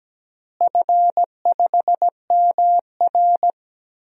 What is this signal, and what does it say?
Morse code, heard as a keyed tone, about 17 wpm, 700 Hz. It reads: F5MR
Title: DXCC-100 Call Signs - 4 Characters - Repeated 3x - 17wpm